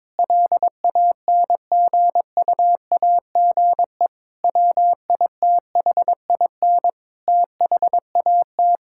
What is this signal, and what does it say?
Morse code, heard as a keyed tone, about 22 wpm, 700 Hz. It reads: LANGUAGE WITHIN THAT
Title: Sentences from Top 1000 Words - Review Part 5 - 22wpm